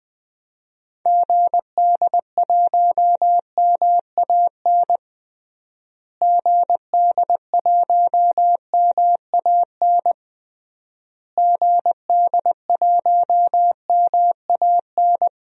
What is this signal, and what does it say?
Morse code, heard as a keyed tone, about 20 wpm, 700 Hz. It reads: GD1MAN GD1MAN GD1MAN
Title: DXCC-100 Call Signs - 6 Characters - Repeated 3x - 20wpm